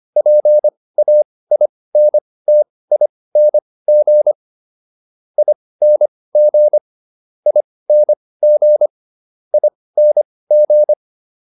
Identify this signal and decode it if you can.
Morse code, heard as a keyed tone, about 25 wpm, spaced as for 18 wpm, 600 Hz. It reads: PAINTING ING ING ING